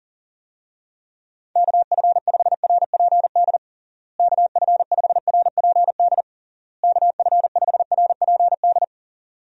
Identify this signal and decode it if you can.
Morse code, heard as a keyed tone, about 40 wpm, 700 Hz. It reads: KF5RPD KF5RPD KF5RPD
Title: USA Call Signs - 6 Characters - Repeated 3x - 40wpm